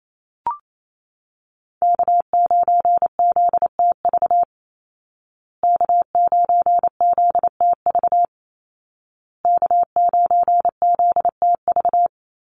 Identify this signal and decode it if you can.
Morse code, heard as a keyed tone, about 28 wpm, 700 Hz. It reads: K9ZTV K9ZTV K9ZTV
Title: Call Signs Repeated 2x - 28wpm